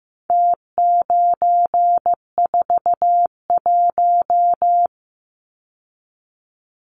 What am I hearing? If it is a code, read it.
Morse code, heard as a keyed tone, about 15 wpm, 700 Hz. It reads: T941
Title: Numbers - Contest Style - 4-Digit 15wpm